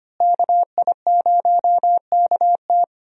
Morse code, heard as a keyed tone, about 25 wpm, 700 Hz. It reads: KI0KT
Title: USA Call Signs - 5 Characters - Repeated 3x - 25wpm